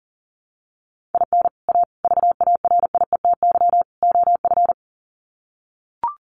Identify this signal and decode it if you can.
Morse code, heard as a keyed tone, about 40 wpm, 700 Hz. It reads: IN A VARIETY OF
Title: Sets of 4 Words 40wpm